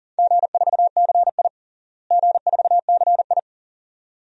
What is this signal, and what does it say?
Morse code, heard as a keyed tone, about 40 wpm, 700 Hz. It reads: G4CI G4CI
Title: DXCC-100 Call Signs - 4 Characters - Repeated 2x - 40wpm